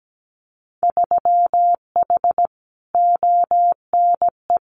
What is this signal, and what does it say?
Morse code, heard as a keyed tone, about 17 wpm, 700 Hz. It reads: 3H ONE
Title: Winter Field Day Exchanges Encore 17wpm